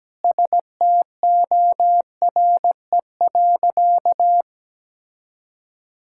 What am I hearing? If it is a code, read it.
Morse code, heard as a keyed tone, about 17 wpm, 700 Hz. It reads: STORE.
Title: Sentences from Top 1000 Words - Review Part 5 - 17wpm